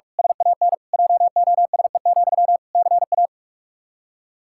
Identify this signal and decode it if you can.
Morse code, heard as a keyed tone, about 46 wpm, 700 Hz. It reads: SAN JOSE, CA
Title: Practice QSO - Exchange 1 - Beginning - 45wpm